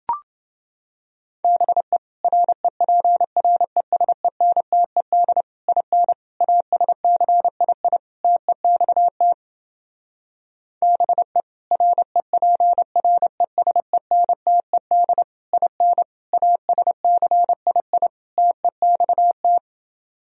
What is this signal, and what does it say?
Morse code, heard as a keyed tone, about 28 wpm, 700 Hz. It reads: BE REPRESENTED IN ASCII TEXT BE REPRESENTED IN ASCII TEXT